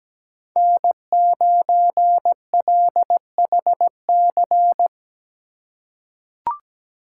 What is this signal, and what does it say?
Morse code, heard as a keyed tone, about 17 wpm, 700 Hz. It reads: N9LHC
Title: USA Call Signs - 5 Characters - 4x Character Spacing - 17wpm